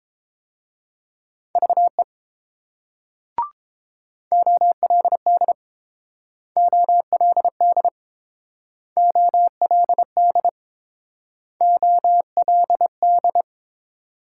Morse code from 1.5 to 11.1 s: VE OLD OLD OLD

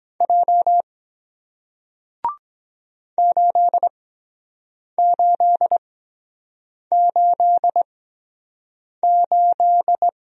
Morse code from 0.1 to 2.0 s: J

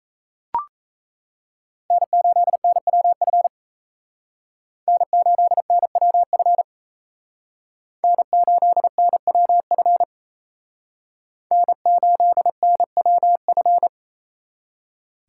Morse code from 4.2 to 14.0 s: N8NWF N8NWF N8NWF

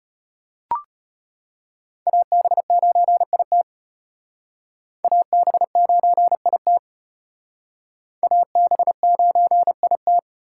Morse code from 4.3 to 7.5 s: AB9IT

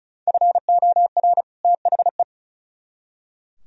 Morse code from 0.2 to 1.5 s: FOR

